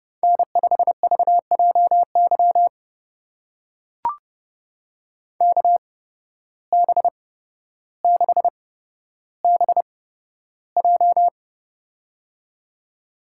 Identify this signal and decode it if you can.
Morse code, heard as a keyed tone, about 32 wpm, spaced as for 13 wpm, 700 Hz. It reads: N5VJY KB6BJ
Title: USA Call Signs - 5 Characters - 8x Character Spacing - 30wpm